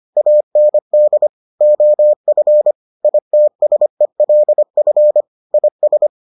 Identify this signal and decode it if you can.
Morse code, heard as a keyed tone, about 25 wpm, 600 Hz. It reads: AND OF ITSELF IS